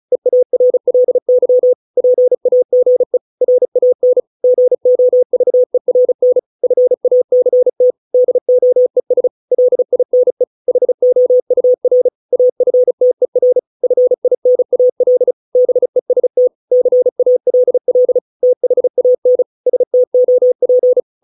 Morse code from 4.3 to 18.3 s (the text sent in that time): GOVERN FACT DOES LINE HOUR AFTER FINAL BEST CALL